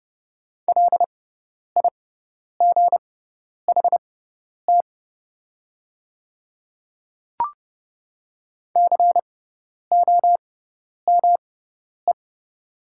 Morse code, heard as a keyed tone, about 30 wpm, spaced as for 10 wpm, 700 Hz. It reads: LIGHT COME
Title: 6x Character Spacing - Top 200 Words - 30wpm